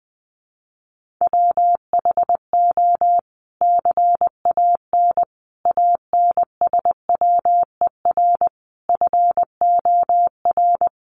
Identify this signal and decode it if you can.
Morse code, heard as a keyed tone, about 20 wpm, 700 Hz. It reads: WHO CAN ANSWER FOR
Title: Sentences from Top 300 Words 20wpm